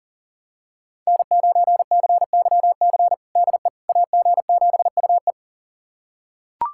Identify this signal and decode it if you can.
Morse code, heard as a keyed tone, about 40 wpm, 700 Hz. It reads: N9CYC DE AG7UE
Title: USA Callsign DE Callsign - 5 Characters - 40wpm